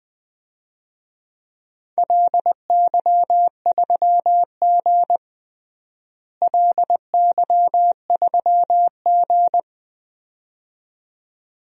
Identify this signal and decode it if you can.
Morse code, heard as a keyed tone, about 20 wpm, 700 Hz. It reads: LY3G LY3G
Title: DXCC-100 Call Signs - 4 Characters - Repeated 2x - 20wpm